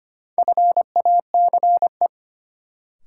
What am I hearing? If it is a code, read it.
Morse code, heard as a keyed tone, about 25 wpm, 700 Hz. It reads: FACE
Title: Sentences from Top 300 Words - New Word Drill - 25wpm